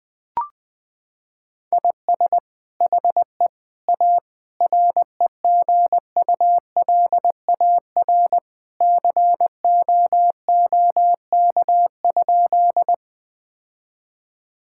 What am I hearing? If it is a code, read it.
Morse code, heard as a keyed tone, about 20 wpm, 700 Hz. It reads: IS HE A REGULAR COOK?
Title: Sentences from Top 1000 Words - Review Part 6 - 20wpm